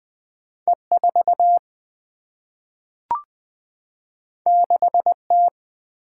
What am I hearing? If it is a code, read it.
Morse code, heard as a keyed tone, about 20 wpm, 700 Hz. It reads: E 4 6 T
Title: Single-Letter Single-Number 20wpm